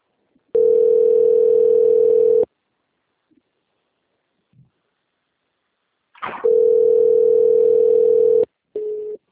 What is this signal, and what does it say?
Ringback tone (North America), 440 + 480 Hz.
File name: out- -2001- - - .41